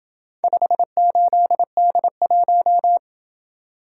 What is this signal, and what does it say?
Morse code, heard as a keyed tone, about 26 wpm, 700 Hz. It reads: 58D1